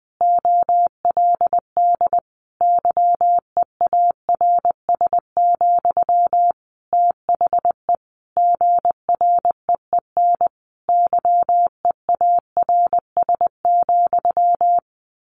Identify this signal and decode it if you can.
Morse code, heard as a keyed tone, about 20 wpm, 700 Hz. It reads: OLD YEARS, THE GREEN YEARS,
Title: A Sound Of Thunder 20wpm